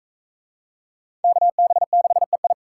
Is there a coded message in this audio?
Yes: Morse code from 1.2 to 2.5 s, heard as a keyed tone, about 40 wpm, 700 Hz. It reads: KB6EI